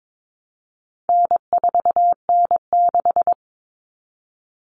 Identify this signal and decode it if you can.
Morse code, heard as a keyed tone, about 22 wpm, 700 Hz. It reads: N4N6
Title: Common Cut Numbers - 4-Digit 22wpm